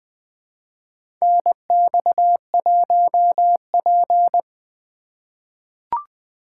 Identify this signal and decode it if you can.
Morse code, heard as a keyed tone, about 20 wpm, 700 Hz. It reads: NX1P
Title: USA Call Signs - 4 Characters - 2x Character Spacing - 20wpm